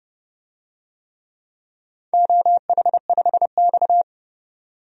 Morse code, heard as a keyed tone, about 28 wpm, 700 Hz. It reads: OH5X
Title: DXCC-100 Call Signs - 4 Characters - 30wpm